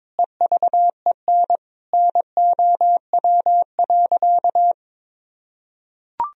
Morse code, heard as a keyed tone, about 22 wpm, 700 Hz. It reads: EVEN NOW.
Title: Sentences from Top 1000 Words - Review Part 1 - 22wpm